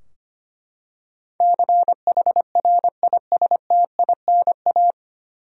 Morse code, heard as a keyed tone, about 25 wpm, 700 Hz. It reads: CHRISTINA
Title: USA Common Names - 25wpm